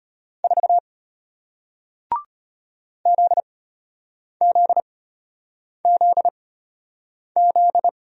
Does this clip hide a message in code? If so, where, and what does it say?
Yes: Morse code from 0.4 to 7.9 s, heard as a keyed tone, about 33 wpm, spaced as for 8 wpm, 700 Hz. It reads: 4 ZZZZ